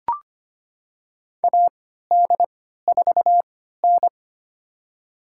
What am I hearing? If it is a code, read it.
Morse code, heard as a keyed tone, about 25 wpm, spaced as for 14 wpm, 700 Hz. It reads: AD4N
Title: USA Call Signs - 4 Characters - 3x Character Spacing - 25wpm